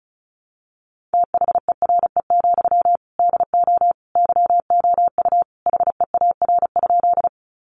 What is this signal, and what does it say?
Morse code, heard as a keyed tone, about 34 wpm, 700 Hz. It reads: THERE, DO YOU HEAR?
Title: Sentences from Top 300 Words 35wpm